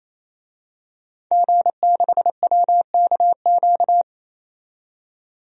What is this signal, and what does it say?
Morse code, heard as a keyed tone, about 28 wpm, 700 Hz. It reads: G6WKQ
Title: DXCC-100 Call Signs - 5 Characters - 28wpm